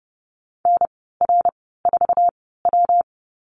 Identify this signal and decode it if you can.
Morse code, heard as a keyed tone, about 30 wpm, spaced as for 17 wpm, 700 Hz. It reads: NR4W
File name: Lesson 19g - Letter w - Call Signs 30wpm